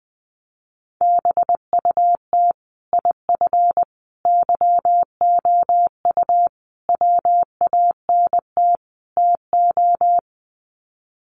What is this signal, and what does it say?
Morse code, heard as a keyed tone, about 20 wpm, 700 Hz. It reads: BUT IF YOU WANT TO
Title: Sets of 5 Words Encore2 20wpm